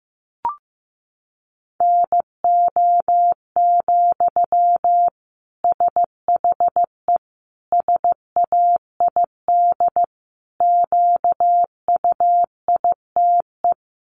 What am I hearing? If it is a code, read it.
Morse code, heard as a keyed tone, about 15 wpm, 700 Hz. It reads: NO, SHE SAID QUITE
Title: Sentences from Top 800 Words 15wpm